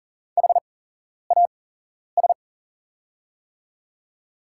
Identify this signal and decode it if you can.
Morse code, heard as a keyed tone, about 39 wpm, spaced as for 11 wpm, 700 Hz. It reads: HAS
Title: 8x Character Spacing - Top 100 Words - 40wpm